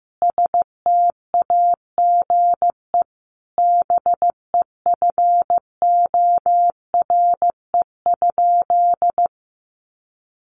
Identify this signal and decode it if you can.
Morse code, heard as a keyed tone, about 15 wpm, 700 Hz. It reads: STAGE BEFORE?
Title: Sentences from Top 1200 Words 15wpm